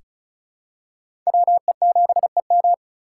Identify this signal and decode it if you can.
Morse code, heard as a keyed tone, about 34 wpm, 700 Hz. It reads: WE7EM
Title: USA Call Signs - 4x Character Spacing - 35wpm